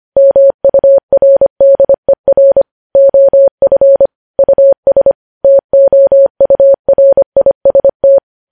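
Morse code, heard as a keyed tone, about 25 wpm, 550 Hz. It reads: MURDER OF US TOURIST
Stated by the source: Morse practice sending news headlines at 25 wpm.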